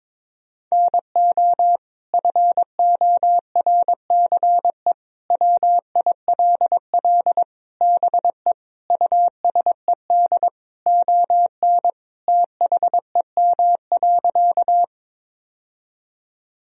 Morse code, heard as a keyed tone, about 22 wpm, 700 Hz. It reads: NO FORCE WILL BE USED ON THEM.
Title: Sentences from Top 1000 Words - Review Part 3 - 22wpm